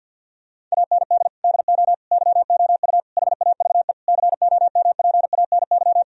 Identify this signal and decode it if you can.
Morse code, heard as a keyed tone, about 50 wpm, 700 Hz. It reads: AND DO YOU HAVE COMPANY
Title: Sentences from Top 1000 Words - Review Part 3 - 50wpm